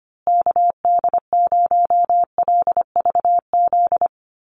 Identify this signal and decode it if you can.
Morse code, heard as a keyed tone, about 25 wpm, 700 Hz. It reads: KD0LVZ
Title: USA Call Signs Repeated 3x 25wpm